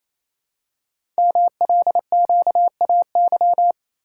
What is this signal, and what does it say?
Morse code, heard as a keyed tone, about 29 wpm, 700 Hz. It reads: MLQAY
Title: Code Groups - 5 Letters - 28wpm